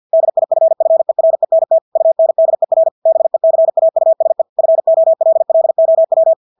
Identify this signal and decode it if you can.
Morse code, heard as a keyed tone, about 50 wpm, 650 Hz. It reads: DIFFERENT UNDER BECAUSE FOLLOW
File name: Engelske ord 50 wpm